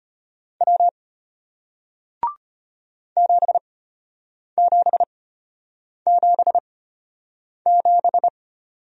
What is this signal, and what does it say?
Morse code, heard as a keyed tone, about 33 wpm, spaced as for 8 wpm, 700 Hz. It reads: W 7777